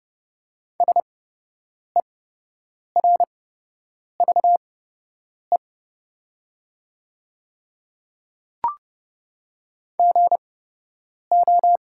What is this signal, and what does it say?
Morse code, heard as a keyed tone, about 30 wpm, spaced as for 8 wpm, 700 Hz. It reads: SERVE GO
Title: 8x Character Spacing - Top 500 Words - 30wpm